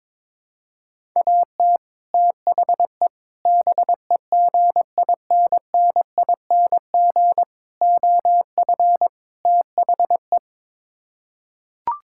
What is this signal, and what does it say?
Morse code, heard as a keyed tone, about 22 wpm, 700 Hz. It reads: AT THE BEGINNING OF THE
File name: practice / 22wpm / Build A Phrase - 5-Word Phrases - 22wpm